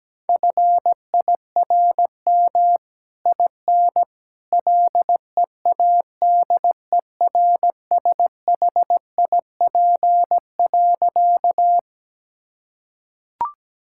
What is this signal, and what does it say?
Morse code, heard as a keyed tone, about 17 wpm, 700 Hz. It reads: FIRM IN LEADERSHIP.
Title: Sentences from Top 1600 Words 17wpm